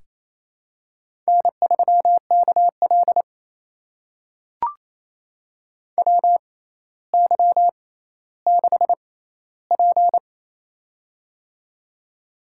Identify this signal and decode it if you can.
Morse code, heard as a keyed tone, about 27 wpm, spaced as for 14 wpm, 700 Hz. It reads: N3KL WY6P
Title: USA Call Signs - 4 Characters - 6x Character Spacing - 28wpm